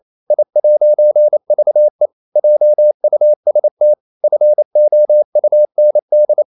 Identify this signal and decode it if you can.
Morse code, heard as a keyed tone, about 28 wpm, 600 Hz. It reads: I'VE JUST FOUND
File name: CW / Practice / EnglishQuotations0228